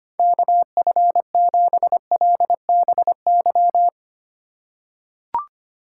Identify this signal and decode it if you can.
Morse code, heard as a keyed tone, about 25 wpm, 700 Hz. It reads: KF7LBY
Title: USA Call Signs - 6 Characters - 3x Character Spacing - 25wpm